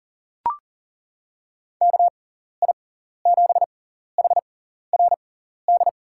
Morse code, heard as a keyed tone, about 40 wpm, spaced as for 14 wpm, 700 Hz. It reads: KI7HRD
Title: USA Call Signs - 6 Characters - 6x Character Spacing - 40wpm